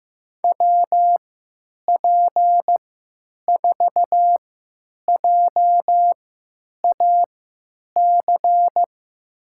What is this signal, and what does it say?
Morse code, heard as a keyed tone, about 15 wpm, spaced as for 9 wpm, 700 Hz. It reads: WP4JAC
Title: USA Call Signs - 3x Character Spacing - 15wpm